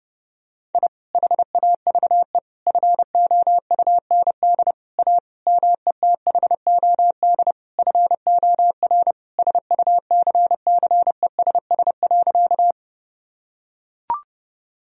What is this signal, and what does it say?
Morse code, heard as a keyed tone, about 32 wpm, 700 Hz. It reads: I HAVE FOUND A METHOD FOR SUCCESS.
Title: Sentences from Top 1000 Words - Review Part 3 - 30wpm